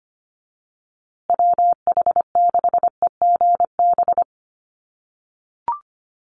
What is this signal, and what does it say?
Morse code, heard as a keyed tone, about 25 wpm, 700 Hz. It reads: WH6EGB
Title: USA Call Signs Repeated 2x 25wpm